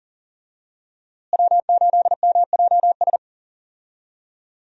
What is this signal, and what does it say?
Morse code, heard as a keyed tone, about 40 wpm, 700 Hz. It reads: W8MJS
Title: USA Call Signs - 5 Characters - Repeated 2x - 40wpm